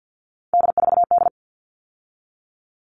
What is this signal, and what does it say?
Morse code, heard as a keyed tone, about 50 wpm, 700 Hz. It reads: D4D
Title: All Cut Numbers - 3-Digit 50wpm